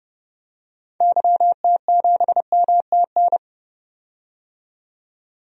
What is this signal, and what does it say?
Morse code, heard as a keyed tone, about 29 wpm, 700 Hz. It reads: YT7MTN
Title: DXCC-100 Call Signs - 6 Characters - 30wpm